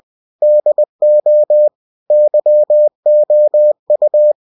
Morse code, heard as a keyed tone, about 20 wpm, 600 Hz. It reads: DO YOU
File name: cw-books / winnie-the-pooh / winnie-20wpm-page-0026